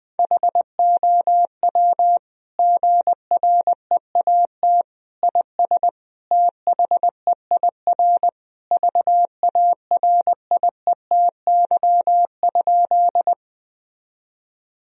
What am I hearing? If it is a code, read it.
Morse code, heard as a keyed tone, about 20 wpm, 700 Hz. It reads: HOW GREAT IS THEIR VARIETY?
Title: Sentences from Top 1000 Words - Review Part 4 - 20wpm